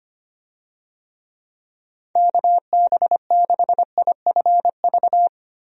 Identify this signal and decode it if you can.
Morse code, heard as a keyed tone, about 25 wpm, 700 Hz. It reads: KB6IFV
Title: USA Call Signs - 6 Characters - Repeated 3x - 25wpm